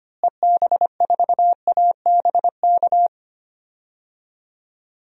Morse code, heard as a keyed tone, about 25 wpm, 700 Hz. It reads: EB4ABK
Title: DXCC-100 Call Signs - 6 Characters - Repeated 3x - 25wpm